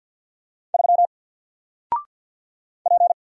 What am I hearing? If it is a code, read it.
Morse code, heard as a keyed tone, about 48 wpm, 700 Hz. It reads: 3 P
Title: Single Character (Letter:Number:Punctuation:Prosign) 50wpm